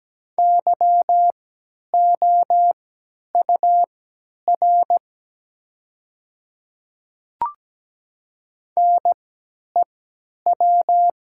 Morse code, heard as a keyed tone, about 17 wpm, spaced as for 10 wpm, 700 Hz. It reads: YOUR NEW
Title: 3x Character Spacing - Top 300 Words - 17wpm